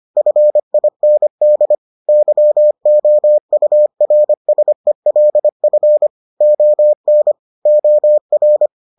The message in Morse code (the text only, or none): FIND YOURSELF ON OR